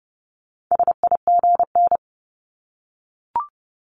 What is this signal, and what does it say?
Morse code, heard as a keyed tone, about 31 wpm, 700 Hz. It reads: SIGN
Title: Multi-letter Phonemes 30wpm